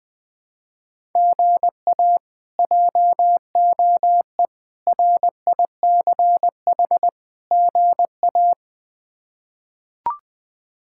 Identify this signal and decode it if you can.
Morse code, heard as a keyed tone, about 20 wpm, 700 Hz. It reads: GA JOE RICH GA
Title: K1USN SST QSO Practice - 20wpm